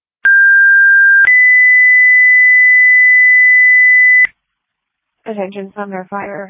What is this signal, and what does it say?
Two-tone page, heard as a one-second tone, then a three-second tone, about 1600 then 2000 Hz, with speech. It is heard from 0.2 to 4.3 s.